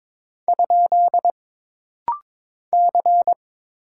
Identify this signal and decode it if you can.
Morse code, heard as a keyed tone, about 22 wpm, 700 Hz. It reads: ? C
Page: This rapid-fire timing is geared to push your ICR (Instant Character Recognition) skills.